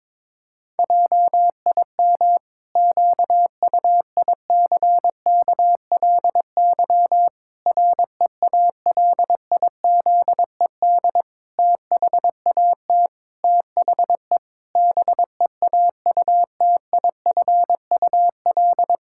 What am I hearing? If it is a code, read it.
Morse code, heard as a keyed tone, about 22 wpm, 700 Hz. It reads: JIM QUICKLY REALIZED THAT THE BEAUTIFUL
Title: Pangram Challenge v2 - 22wpm